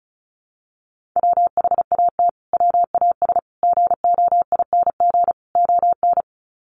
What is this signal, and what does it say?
Morse code, heard as a keyed tone, about 34 wpm, 700 Hz. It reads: WHAT WAS GOING ON
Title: Sets of 4 Words 35wpm